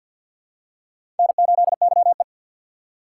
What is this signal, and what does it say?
Morse code, heard as a keyed tone, about 52 wpm, 700 Hz. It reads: N8YE
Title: USA Call Signs - 2x Character Spacing - 50wpm